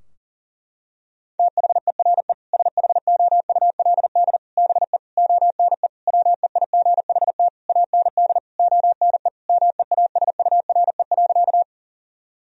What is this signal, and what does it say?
Morse code, heard as a keyed tone, about 40 wpm, 700 Hz. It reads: THERE SHOULD BE ONE WEIGHT AND ONE MEASURE.
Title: Sentences from Top 1000 Words - Review Part 3 - 40wpm